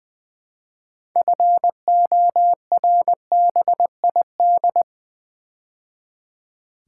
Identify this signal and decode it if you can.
Morse code, heard as a keyed tone, about 20 wpm, 690 Hz. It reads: FORBID
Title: Build A Word - 2 Letters At A Time - 6-Letter Words - 20wpm